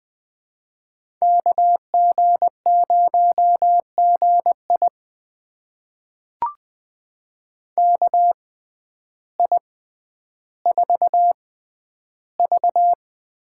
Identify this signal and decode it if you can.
Morse code, heard as a keyed tone, about 20 wpm, spaced as for 11 wpm, 700 Hz. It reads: KG0GI KI4V
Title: USA Call Signs - 6x Character Spacing - 20wpm